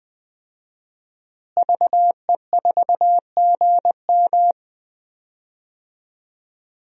Morse code, heard as a keyed tone, about 20 wpm, 700 Hz. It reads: VE4GM
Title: DXCC-100 Call Signs - 5 Characters - Repeated 3x - 20wpm